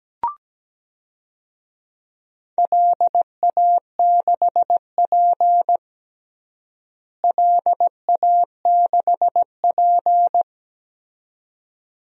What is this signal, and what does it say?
Morse code, heard as a keyed tone, about 17 wpm, 700 Hz. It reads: LA6P LA6P
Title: DXCC-100 Call Signs - 4 Characters - Repeated 2x - 17wpm